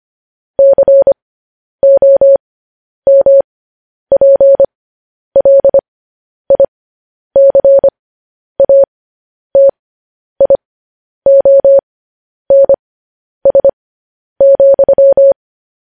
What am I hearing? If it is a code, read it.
Morse code, heard as a keyed tone, about 25 wpm, spaced as for 10 wpm, 550 Hz. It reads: COMPLICATIONS,
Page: News Headlines in Morse Code at 10 WPM